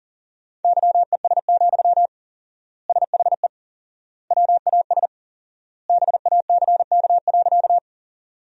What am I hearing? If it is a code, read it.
Morse code, heard as a keyed tone, about 37 wpm, 700 Hz. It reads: YES, SHE WAS BACK.
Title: 4x Word-Spacing - Sentences from Top 500 Words - 40wpm